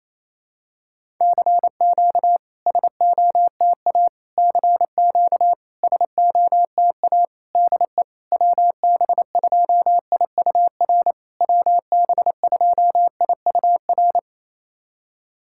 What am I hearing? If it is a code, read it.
Morse code, heard as a keyed tone, about 28 wpm, 700 Hz. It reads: CQ SOTA CQ SOTA DE WB2IUR WB2IUR
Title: SOTA QSO Practice 28wpm